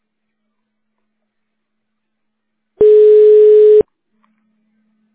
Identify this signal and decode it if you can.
Ringback tone (Europe, 425 Hz).